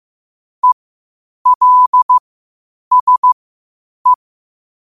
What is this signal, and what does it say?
Morse code, heard as a keyed tone, about 15 wpm, spaced as for 8 wpm, 1000 Hz. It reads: ELSE
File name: qotdfiles / 8wpm